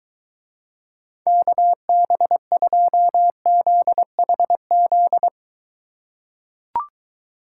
Morse code, heard as a keyed tone, about 22 wpm, 700 Hz. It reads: KB2ZHZ